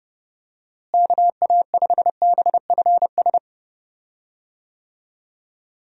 Morse code, heard as a keyed tone, about 30 wpm, 700 Hz. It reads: KA5BFS
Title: USA Call Signs - 6 Characters - Repeated 2x - 30wpm